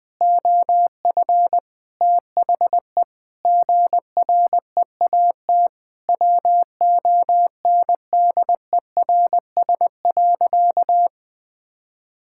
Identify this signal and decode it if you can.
Morse code, heard as a keyed tone, about 20 wpm, 700 Hz. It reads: OF THE GREAT WONDERS.
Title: Sentences with 4-Word Phrases - Spoken Beginning - 20wpm